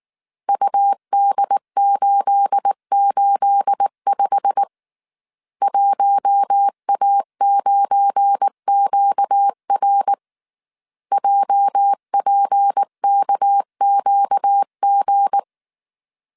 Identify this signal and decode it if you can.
Morse code, heard as a keyed tone, about 19 wpm, 800 Hz. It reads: UD885 1A9QR JPKQG
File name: CW / DET / 1 9 1500C S A 18wpm 2459